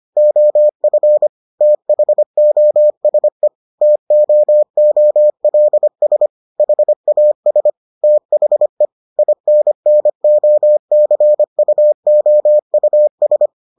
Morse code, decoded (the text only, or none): OF THOSE TOOLS HAS THE INNOCUOUS